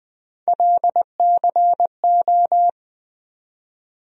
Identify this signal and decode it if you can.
Morse code, heard as a keyed tone, about 20 wpm, 700 Hz. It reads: LCO
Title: USA Call Sign Postfixes - 3 Letters 20wpm